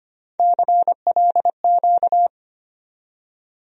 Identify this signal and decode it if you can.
Morse code, heard as a keyed tone, about 25 wpm, 700 Hz. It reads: CLQ
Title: USA Call Sign Postfixes - 3 Letters 25wpm